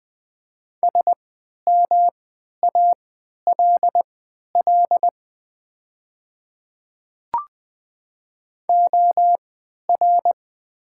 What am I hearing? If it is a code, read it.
Morse code, heard as a keyed tone, about 20 wpm, spaced as for 11 wpm, 700 Hz. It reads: SMALL OR
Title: 3x Character Spacing - Top 200 Words - 20wpm